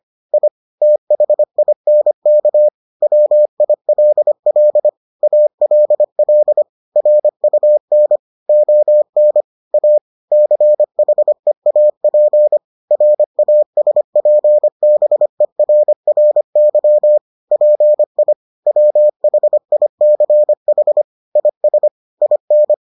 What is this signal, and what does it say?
Morse code, heard as a keyed tone, about 25 wpm, 600 Hz. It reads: I THINK WILL ALL RUN ON A CHEAP RASPBERRY PI WHICH IS IN